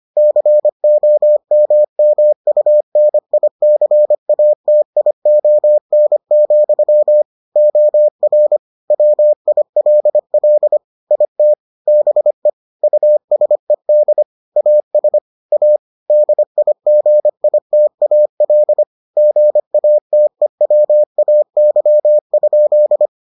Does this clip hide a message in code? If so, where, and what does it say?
Yes: Morse code from 0.2 to 23.1 s, heard as a keyed tone, about 25 wpm, 600 Hz. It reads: COMMUNICATION, OR WILL IT BE USED AS A DIGITAL GATEWAY?